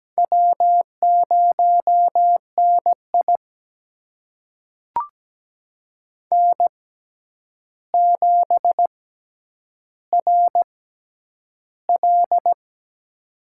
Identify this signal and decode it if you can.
Morse code, heard as a keyed tone, about 17 wpm, spaced as for 9 wpm, 700 Hz. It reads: W0NI N7RL
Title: USA Call Signs - 4 Characters - 6x Character Spacing - 17wpm